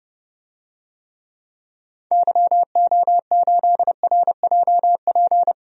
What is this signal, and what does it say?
Morse code, heard as a keyed tone, about 30 wpm, 700 Hz. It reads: YO8RJP
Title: DXCC-100 Call Signs - 6 Characters - 30wpm